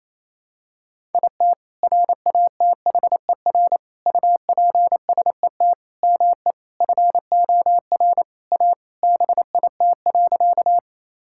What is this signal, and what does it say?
Morse code, heard as a keyed tone, about 28 wpm, 700 Hz. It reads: IT RATHER UPSET ME FOR A BIT.
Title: Sentences from Top 1000 Words - Review Part 5 - 28wpm